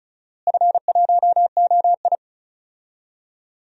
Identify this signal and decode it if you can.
Morse code, heard as a keyed tone, about 34 wpm, 700 Hz. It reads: F1OI
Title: DXCC-100 Call Signs - 4 Characters - Repeated 3x - 35wpm